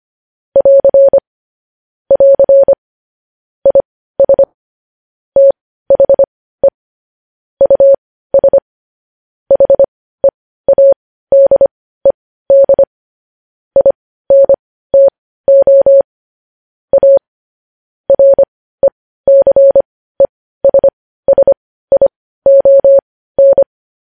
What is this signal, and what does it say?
Morse code, heard as a keyed tone, about 25 wpm, spaced as for 15 wpm, 550 Hz. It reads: <AR> <AR> IS THE US HEADED INTO A RECESSION
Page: news headlines in morse code at 15 wpm